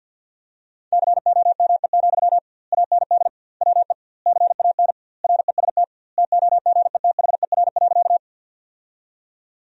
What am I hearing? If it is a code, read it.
Morse code, heard as a keyed tone, about 50 wpm, 700 Hz. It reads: COME, AND WE CAN REST TOGETHER.
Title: 2x Word-Spacing - Sentences from Top 500 Words - 50wpm